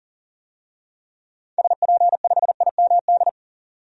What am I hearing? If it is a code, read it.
Morse code, heard as a keyed tone, about 40 wpm, 700 Hz. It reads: SP5IMD
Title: DXCC-100 Call Signs - 6 Characters - 40wpm